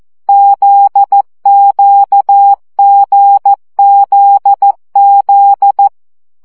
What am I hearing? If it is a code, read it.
Morse code, heard as a keyed tone, about 14 wpm, 800 Hz. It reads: ZQGZZ
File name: CW gyakorlas g z q 12 baud